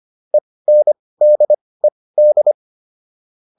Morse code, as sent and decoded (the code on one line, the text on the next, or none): . -. -.. -..
ENDED